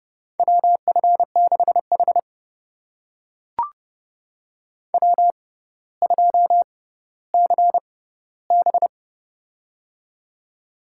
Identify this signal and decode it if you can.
Morse code, heard as a keyed tone, about 30 wpm, spaced as for 15 wpm, 700 Hz. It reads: WF6H W2CB
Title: USA Call Signs - 6x Character Spacing - 30wpm